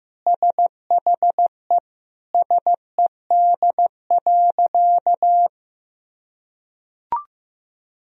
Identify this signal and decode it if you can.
Morse code, heard as a keyed tone, about 15 wpm, 700 Hz. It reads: SHE SED.
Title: Sentences from Top 300 Words and Abbreviations 15wpm